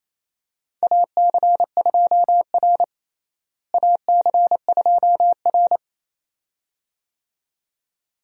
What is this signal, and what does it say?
Morse code, heard as a keyed tone, about 28 wpm, 700 Hz. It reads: AC2R AC2R
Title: USA Call Signs - 4 Characters - Repeated 2x - 28wpm